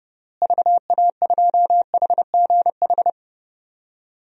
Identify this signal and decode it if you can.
Morse code, heard as a keyed tone, about 32 wpm, 700 Hz. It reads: VA2HGH
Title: DXCC-100 Call Signs - 6 Characters - Repeated 2x - 30wpm